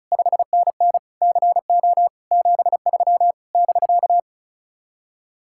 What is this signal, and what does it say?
Morse code, heard as a keyed tone, about 43 wpm, 700 Hz. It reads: 5NN CO 73 <BK>
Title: Speed-Racing Common QSO Phrases - 35wpm